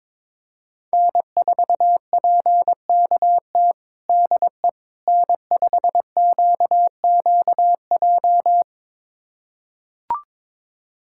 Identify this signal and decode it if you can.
Morse code, heard as a keyed tone, about 22 wpm, 700 Hz. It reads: N4PKT DE N5QQJ
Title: USA Callsign DE Callsign - 5 Characters - 22wpm